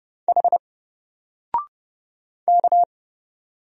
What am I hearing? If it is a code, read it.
Morse code, heard as a keyed tone, about 30 wpm, 700 Hz. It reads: H K
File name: Single-Letter-Number-30wpm